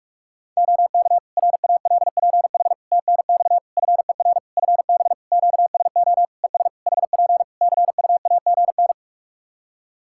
Morse code, heard as a keyed tone, about 46 wpm, 700 Hz. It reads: OK RALPH TNX FER FB QSO ES HP CUAGN
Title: Practice QSO - Exchange 4 - Ending - 45wpm